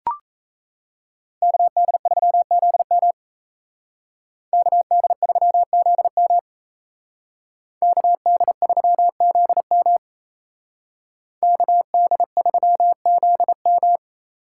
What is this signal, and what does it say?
Morse code, heard as a keyed tone, about 36 wpm, 700 Hz. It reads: KD3ZM KD3ZM KD3ZM KD3ZM